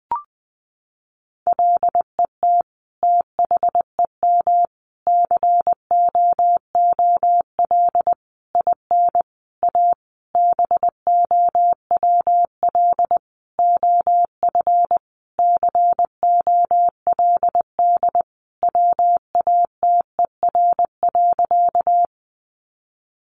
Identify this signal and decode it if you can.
Morse code, heard as a keyed tone, about 20 wpm, 700 Hz. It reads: LET THEM COOL IN A BOWL OF COLD WATER.
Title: Sentences from Top 1200 Words 20wpm